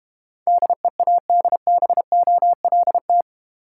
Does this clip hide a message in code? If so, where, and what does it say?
Yes: Morse code from 0.5 to 3.2 s, heard as a keyed tone, about 32 wpm, 700 Hz. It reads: DEADBOLT